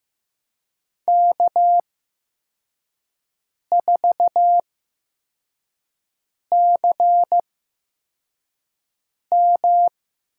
Morse code, heard as a keyed tone, about 15 wpm, spaced as for 4 wpm, 700 Hz. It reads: K4CM
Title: USA Call Signs - 8x Character Spacing - 15wpm